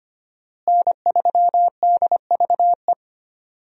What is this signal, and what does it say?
Morse code, heard as a keyed tone, about 25 wpm, 700 Hz. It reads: N3DVE
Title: USA Call Signs - 5 Characters - Repeated 3x - 25wpm